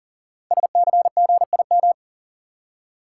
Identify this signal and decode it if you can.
Morse code, heard as a keyed tone, about 42 wpm, 700 Hz. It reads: SCGIM
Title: Code Groups - 5 Letter:Numbers - 40wpm